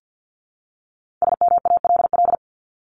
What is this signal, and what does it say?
Morse code, heard as a keyed tone, about 48 wpm, 700 Hz. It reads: SMALL
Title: Top 200 Words 50wpm